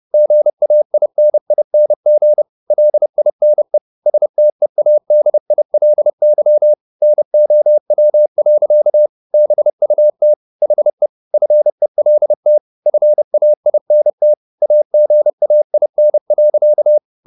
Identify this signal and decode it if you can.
Morse code, heard as a keyed tone, about 30 wpm, 600 Hz. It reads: GAINING LINE STEADILY NOW. BUT HE FELT FAINT AGAIN.